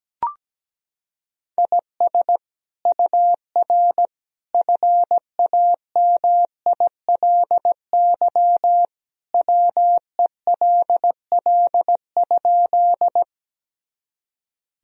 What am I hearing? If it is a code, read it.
Morse code, heard as a keyed tone, about 17 wpm, 700 Hz. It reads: IS UR FAMILY WELL?
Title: Sentences from Top 400 Words and Abbreviations 17wpm